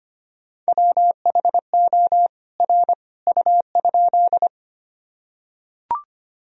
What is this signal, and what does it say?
Morse code, heard as a keyed tone, about 25 wpm, 700 Hz. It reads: WHO R U?
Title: Sentences from Top 100 Words and Abbreviations 25wpm